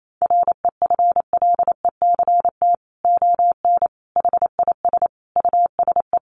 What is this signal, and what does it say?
Morse code, heard as a keyed tone, about 28 wpm, 700 Hz. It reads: REFLECT ON HIS USE
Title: Sentences from Top 1500 Words 28wpm